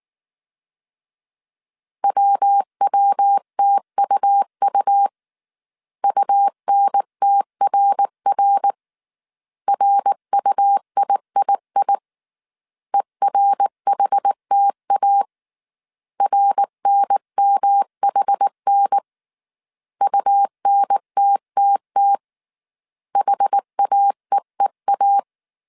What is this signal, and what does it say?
Morse code, heard as a keyed tone, about 19 wpm, 800 Hz. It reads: WWTUU UNTRR RUIII ERHTA RNMHN UNTTT HAEEA
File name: CW / DET / 1 3 1500C S B 18wpm 1916